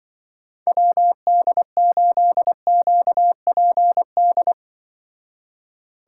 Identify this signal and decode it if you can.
Morse code, heard as a keyed tone, about 24 wpm, 700 Hz. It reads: WD8QPD
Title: Speed Racing - USA 2x3 Call Signs 20wpm